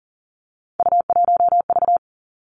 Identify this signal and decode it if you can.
Morse code, heard as a keyed tone, about 40 wpm, 700 Hz. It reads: U1V
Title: Three Challenging Letter-Number Combinations 40wpm